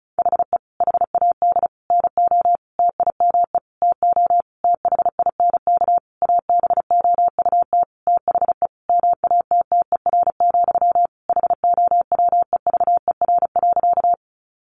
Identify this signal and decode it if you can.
Morse code, heard as a keyed tone, about 34 wpm, 700 Hz. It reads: HE HAD NO TIME TO THINK ABOUT THE MATTER, HOWEVER.
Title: Sentences from Top 700 Words 35wpm